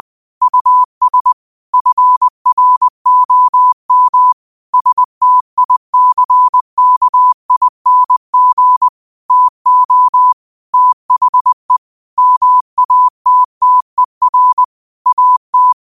Morse code, heard as a keyed tone, about 20 wpm, 1000 Hz. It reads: US FROM STICKING TO THE MATTER AT